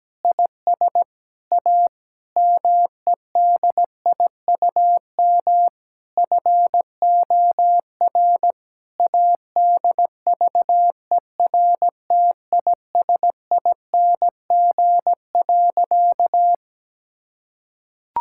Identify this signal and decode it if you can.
Morse code, heard as a keyed tone, about 17 wpm, 700 Hz. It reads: IS A MEDIUM FOR ADVERTISING.
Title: Sentences from Top 1000 Words - Review Part 6 - 17wpm